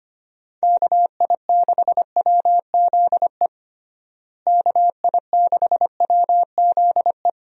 Morse code, heard as a keyed tone, about 25 wpm, 700 Hz. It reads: KI6WZE KI6WZE
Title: USA Call Signs - 6 Characters - Repeated 2x - 25wpm